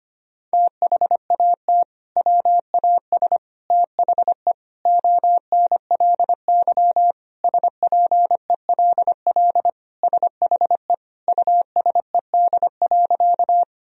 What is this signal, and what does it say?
Morse code, heard as a keyed tone, about 25 wpm, 700 Hz. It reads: THAT WAS THE ONLY SPELL SHE USED.
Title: Sentences from Top 1000 Words - Review Part 1 - 25wpm